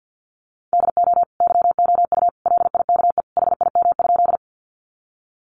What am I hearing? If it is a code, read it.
Morse code, heard as a keyed tone, about 50 wpm, 700 Hz. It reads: DO YOU LIKE HIM?
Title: Sentences from Top 100 Words 50wpm